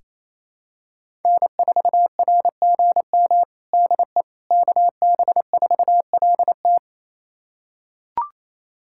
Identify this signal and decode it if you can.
Morse code, heard as a keyed tone, about 28 wpm, 700 Hz. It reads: N4RGM DE KB4LT
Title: USA Callsign DE Callsign - 5 Characters - 28wpm